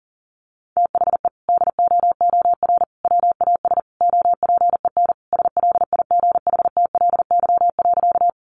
Morse code, heard as a keyed tone, about 40 wpm, 700 Hz. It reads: THE DOOR WAS OPEN SLIGHTLY.
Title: Sentences from Top 1300 Words 40wpm